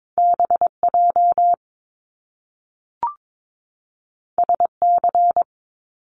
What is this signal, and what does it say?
Morse code, heard as a keyed tone, about 22 wpm, 700 Hz. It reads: BJ SC
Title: Top Two Letter Combinations in Words 22wpm